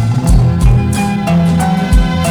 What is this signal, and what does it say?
There is no code.